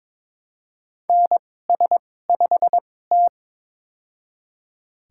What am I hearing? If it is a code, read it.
Morse code, heard as a keyed tone, about 22 wpm, spaced as for 16 wpm, 700 Hz. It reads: NS5T
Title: USA Call Signs - 4 Characters - 2x Character Spacing - 22wpm